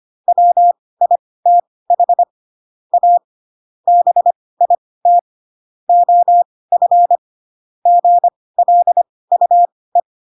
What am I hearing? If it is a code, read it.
Morse code, heard as a keyed tone, about 25 wpm, spaced as for 18 wpm, 700 Hz. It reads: WITH A BIT OF GLUE